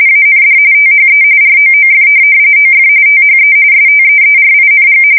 RTTY, about 45.45 baud. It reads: AND THE ROOF OF THE STEERING
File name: fsk170 / USS-Missouri